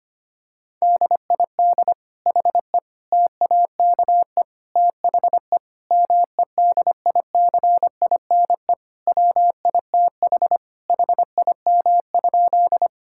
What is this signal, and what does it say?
Morse code, heard as a keyed tone, about 25 wpm, 700 Hz. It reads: DID HE TAKE THE MEDICINE WITH HIM?
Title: Sentences from Top 1000 Words - Review Part 6 - 25wpm